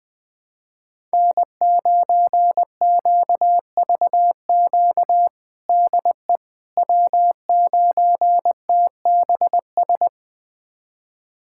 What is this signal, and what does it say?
Morse code, heard as a keyed tone, about 20 wpm, 700 Hz. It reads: N9QVQ DE W9TBS
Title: USA Callsign DE Callsign - 5 Characters - 20wpm